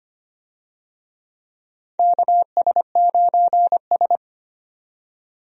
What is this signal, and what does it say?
Morse code, heard as a keyed tone, about 25 wpm, 700 Hz. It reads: KS9S
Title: USA Call Signs - 4 Characters - 25wpm